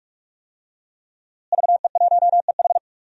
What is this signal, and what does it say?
Morse code, heard as a keyed tone, about 48 wpm, 700 Hz. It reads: VE1EH
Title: DXCC-100 Call Signs - 5 Characters - Repeated 2x - 45wpm